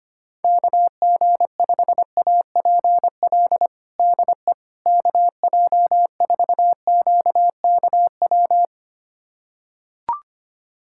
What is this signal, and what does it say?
Morse code, heard as a keyed tone, about 25 wpm, 700 Hz. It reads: KG5APL DE KJ4QKW
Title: USA Callsign DE Callsign 25wpm